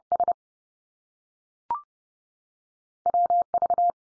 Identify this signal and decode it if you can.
Morse code, heard as a keyed tone, about 30 wpm, 700 Hz. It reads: S WV
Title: US States 30WPM